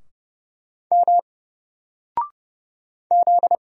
Morse code, heard as a keyed tone, about 30 wpm, 700 Hz. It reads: M Z
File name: Single-Letter-Number-30wpm